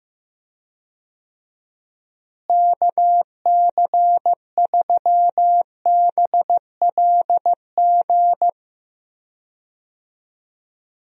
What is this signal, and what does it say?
Morse code, heard as a keyed tone, about 15 wpm, 700 Hz. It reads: KC3BLG
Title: USA Call Signs - 6 Characters - Repeated 3x - 15wpm